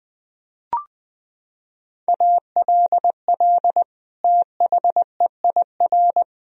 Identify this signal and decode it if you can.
Morse code, heard as a keyed tone, about 20 wpm, 700 Hz. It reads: ALL THEIR
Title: Sentences from Top 1000 Words - Review Part 2 - 20wpm